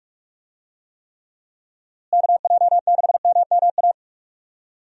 Morse code, heard as a keyed tone, about 42 wpm, 700 Hz. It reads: KJ6MMA
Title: USA Call Signs - 6 Characters - Repeated 3x - 45wpm